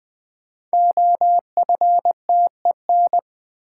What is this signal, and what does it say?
Morse code, heard as a keyed tone, about 20 wpm, 700 Hz. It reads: OFTEN
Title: Sentences from Top 300 Words - New Word Drill - 20wpm